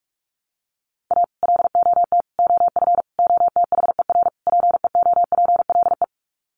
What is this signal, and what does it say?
Morse code, heard as a keyed tone, about 44 wpm, 700 Hz. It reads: A LOT OF OTHER PEOPLE
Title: Sets of 5 Words Encore2 45wpm